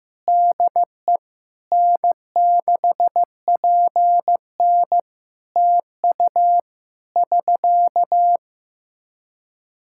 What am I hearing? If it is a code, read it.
Morse code, heard as a keyed tone, about 15 wpm, 700 Hz. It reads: DE N6PN TU <SK>
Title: Practice QSO - Exchange 4 - Ending - 15wpm